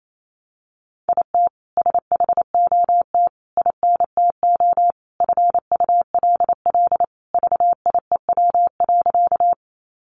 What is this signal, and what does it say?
Morse code, heard as a keyed tone, about 28 wpm, 700 Hz. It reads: IT SHOT INTO FULL VIEW.
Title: Sentences from Top 1100 Words 28wpm